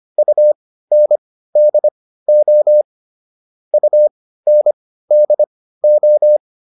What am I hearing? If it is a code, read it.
Morse code, heard as a keyed tone, about 25 wpm, spaced as for 15 wpm, 600 Hz. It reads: UNDO UNDO